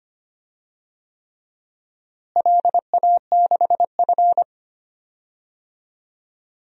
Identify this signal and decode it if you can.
Morse code, heard as a keyed tone, about 25 wpm, 700 Hz. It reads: LA6F
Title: DXCC-100 Call Signs - 4 Characters - Repeated 3x - 25wpm